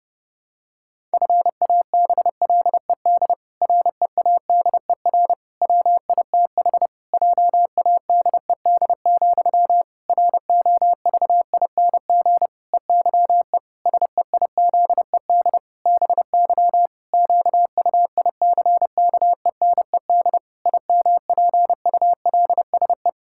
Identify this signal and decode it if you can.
Morse code, heard as a keyed tone, about 30 wpm, 700 Hz. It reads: FABLED READER WITH JADED, ROVING EYE SEIZED BY QUICKENED IMPULSE
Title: Pangram Challenge v2 - 30wpm